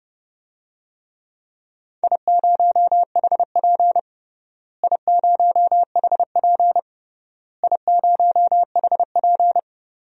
Morse code, heard as a keyed tone, about 30 wpm, 700 Hz. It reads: I0HP I0HP I0HP
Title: DXCC-100 Call Signs - 4 Characters - Repeated 3x - 30wpm